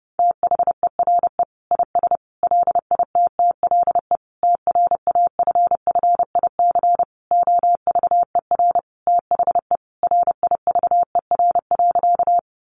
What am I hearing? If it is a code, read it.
Morse code, heard as a keyed tone, about 30 wpm, 700 Hz. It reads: THERE IS LITTLE TRAFFIC OVER THE RIVER.
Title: Sentences from Top 1300 Words 30wpm